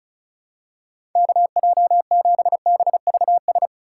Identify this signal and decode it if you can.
Morse code, heard as a keyed tone, about 36 wpm, 700 Hz. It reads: KJ7BVS
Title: USA Call Signs - 6 Characters - 3x Character Spacing - 35wpm